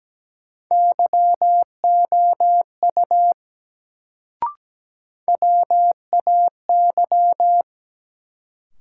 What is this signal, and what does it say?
Morse code, heard as a keyed tone, about 17 wpm, 700 Hz. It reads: YOU WAY
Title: Sentences from Top 100 Words - New Word Drill - 17wpm